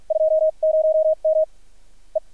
Morse code, heard as a keyed tone, about 53 wpm, spaced as for 40 wpm, 630 Hz. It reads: 20M E
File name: archived / CRSnet